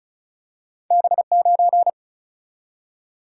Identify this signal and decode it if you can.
Morse code, heard as a keyed tone, about 36 wpm, 700 Hz. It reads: B 9
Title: Single-Letter Single-Number 35wpm